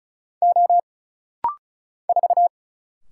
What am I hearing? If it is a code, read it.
Morse code, heard as a keyed tone, about 36 wpm, 700 Hz. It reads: O 4
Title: Single Letter-Number Alt - Rapid-Fire 35wpm